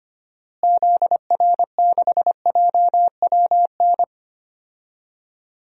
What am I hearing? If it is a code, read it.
Morse code, heard as a keyed tone, about 25 wpm, 700 Hz. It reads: ZR6JWN
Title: DXCC-100 Call Signs - 6 Characters - Repeated 2x - 25wpm